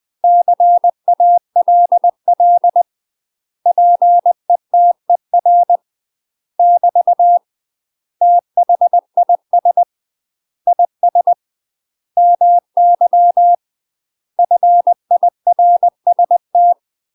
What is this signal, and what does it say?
Morse code, heard as a keyed tone, about 20 wpm, 700 Hz. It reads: CALL PETER <BT> THIS IS MY FIRST
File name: UK MASTER QSOs TRAINING / 20WPM / UK MASTER QSOs 0005